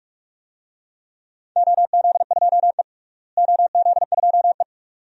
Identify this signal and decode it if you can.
Morse code, heard as a keyed tone, about 46 wpm, 700 Hz. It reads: OZ2E OZ2E
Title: DXCC-100 Call Signs - 4 Characters - Repeated 2x - 45wpm